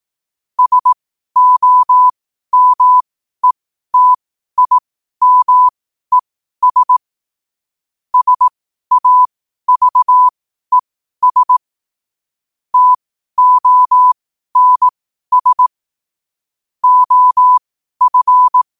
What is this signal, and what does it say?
Morse code, heard as a keyed tone, about 18 wpm, spaced as for 12 wpm, 1000 Hz. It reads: SOMETIMES SAVES TONS OF
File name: qotdfiles / 12wpm